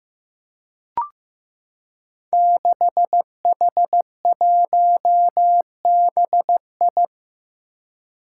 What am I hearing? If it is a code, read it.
Morse code, heard as a keyed tone, about 15 wpm, 700 Hz. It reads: 6H1BI
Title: Five Challenging Letter-Number Combinations 15wpm